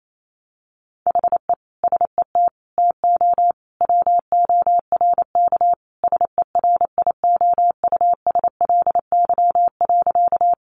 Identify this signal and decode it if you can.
Morse code, heard as a keyed tone, about 28 wpm, 700 Hz. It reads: HE SET TO WORK SERIOUSLY.
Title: Sentences from Top 1200 Words 28wpm